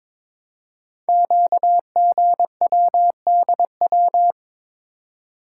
Morse code, heard as a keyed tone, about 22 wpm, 700 Hz. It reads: QGWDW
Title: Code Groups - 5 Letters - 22wpm